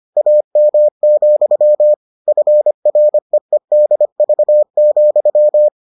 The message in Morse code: AM, FREEDV,

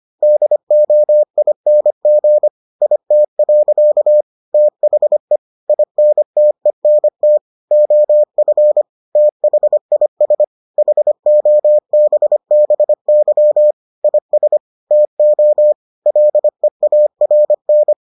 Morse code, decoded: DOING IT. THE INTENT OF THIS HOBBY IS TO LEARN

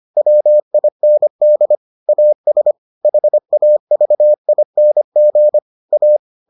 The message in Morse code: WIND AS HAVING A

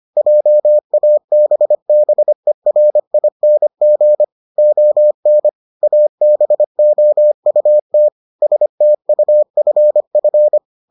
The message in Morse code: JABBERING ON ABOUT STUFF

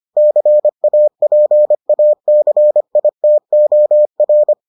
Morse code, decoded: CAPACITOR

